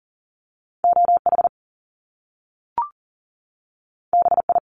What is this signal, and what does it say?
Morse code, heard as a keyed tone, about 40 wpm, 700 Hz. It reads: OH BI
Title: Top Two Letter Combinations in Words 40wpm